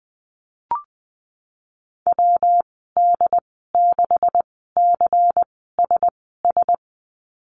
Morse code, heard as a keyed tone, about 20 wpm, spaced as for 15 wpm, 700 Hz. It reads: WD6CSS
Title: 147 Lesson 24g - Letter y - Call Signs 20wpm